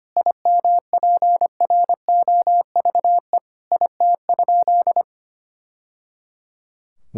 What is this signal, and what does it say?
Morse code, heard as a keyed tone, about 25 wpm, 700 Hz. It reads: IMPROVE IT?
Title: Sentences from Top 1000 Words - Review Part 5 - 25wpm